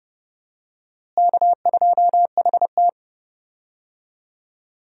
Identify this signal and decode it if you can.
Morse code, heard as a keyed tone, about 31 wpm, 700 Hz. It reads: K2HT
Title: USA Call Signs - 4 Characters - Repeated 3x - 30wpm